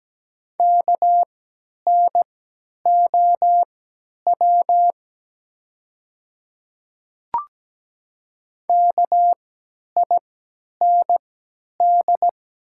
Morse code, heard as a keyed tone, about 17 wpm, spaced as for 10 wpm, 700 Hz. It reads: KNOW KIND
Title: 3x Character Spacing - Top 200 Words - 17wpm